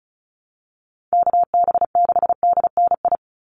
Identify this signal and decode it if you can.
Morse code, heard as a keyed tone, about 34 wpm, 700 Hz. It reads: KB6DNI
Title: USA Call Signs 35wpm